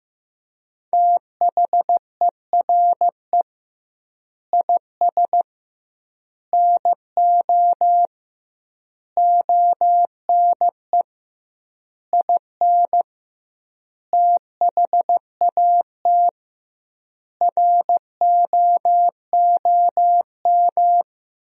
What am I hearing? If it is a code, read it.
Morse code, heard as a keyed tone, about 15 wpm, 700 Hz. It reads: THERE IS NO ONE IN THAT ROOM
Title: 2x Word-Spacing - Sentences from Top 500 Words - 15wpm